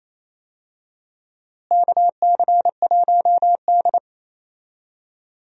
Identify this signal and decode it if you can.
Morse code, heard as a keyed tone, about 28 wpm, 700 Hz. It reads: KC1D
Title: USA Call Signs - 4 Characters - Repeated 2x - 28wpm